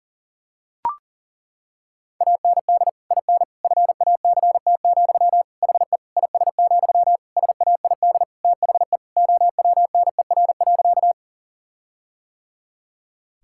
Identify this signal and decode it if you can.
Morse code, heard as a keyed tone, about 40 wpm, 700 Hz. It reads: AND IN FACT, HE IS, SAID THE OWNER.
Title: Sentences from Top 1600 Words 40wpm